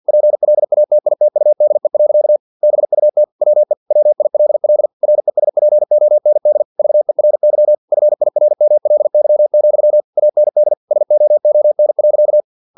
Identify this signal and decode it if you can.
Morse code, heard as a keyed tone, about 50 wpm, 600 Hz. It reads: PLATITUDE. BUT WE WILL RESPOND VERY FIRMLY, AND SOON.
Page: Never 49wpm